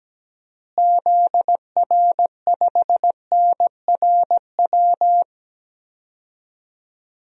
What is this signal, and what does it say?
Morse code, heard as a keyed tone, about 17 wpm, 700 Hz. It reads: ZR5NRW
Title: DXCC-100 Call Signs - 6 Characters - Repeated 3x - 17wpm